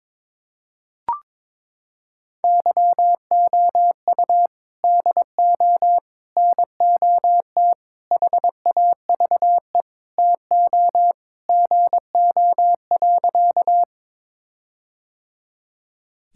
Morse code, heard as a keyed tone, about 21 wpm, 700 Hz. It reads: YOU DO NOT HAVE TO GO.
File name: practice / 22wpm / Sentences with 4-Word Phrases - 22wpm